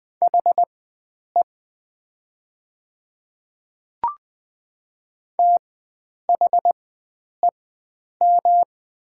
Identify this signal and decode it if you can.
Morse code, heard as a keyed tone, about 20 wpm, spaced as for 9 wpm, 690 Hz. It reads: HE THEM
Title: 4x Character Spacing - Top 100 Words - 20wpm